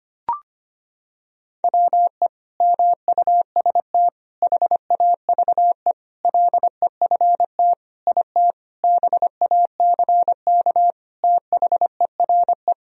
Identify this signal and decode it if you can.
Morse code, heard as a keyed tone, about 25 wpm, 700 Hz. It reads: WE MUST HAVE LEFT IT BACK THERE
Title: Sentences from Top 1000 Words - Review Part 2 - 25wpm